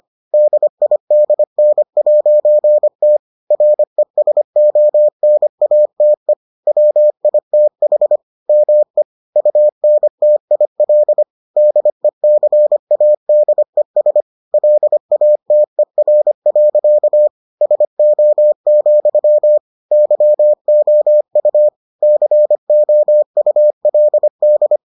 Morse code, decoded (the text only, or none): DIDN'T RESONATE WITH ME UNTIL DECADES LATER. SO, YOU COULD